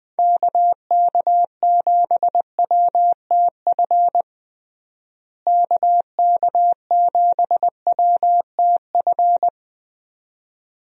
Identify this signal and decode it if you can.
Morse code, heard as a keyed tone, about 20 wpm, 700 Hz. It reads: KK7WTF KK7WTF
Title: USA Call Signs - 6 Characters - Repeated 2x - 20wpm